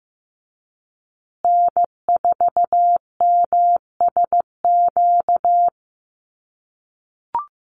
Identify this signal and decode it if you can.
Morse code, heard as a keyed tone, about 15 wpm, 700 Hz. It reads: N4MSQ
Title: USA Call Signs Repeated 2x 15wpm